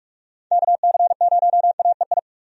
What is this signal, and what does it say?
Morse code, heard as a keyed tone, about 46 wpm, 700 Hz. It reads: KC0AEI
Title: USA Call Signs - 6 Characters - Repeated 2x - 45wpm